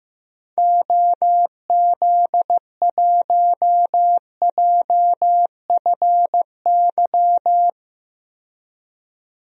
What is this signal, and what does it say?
Morse code, heard as a keyed tone, about 15 wpm, 700 Hz. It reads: OZ1JFY
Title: DXCC-100 Call Signs - 6 Characters - 15wpm